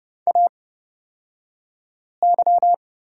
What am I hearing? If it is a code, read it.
Morse code, heard as a keyed tone, about 30 wpm, spaced as for 5 wpm, 700 Hz. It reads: AY
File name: Lesson 37d - Letter x - Two Characters 30wpm